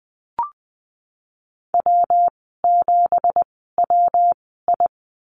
Morse code, heard as a keyed tone, about 20 wpm, spaced as for 15 wpm, 700 Hz. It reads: W7WI
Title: 193 Lesson 31g - Letter v - Call Signs 20wpm